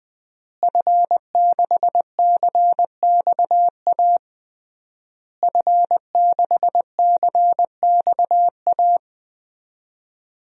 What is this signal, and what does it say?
Morse code, heard as a keyed tone, about 21 wpm, 700 Hz. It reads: F6CXA F6CXA
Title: DXCC-100 Call Signs - 5 Characters - Repeated 3x - 20wpm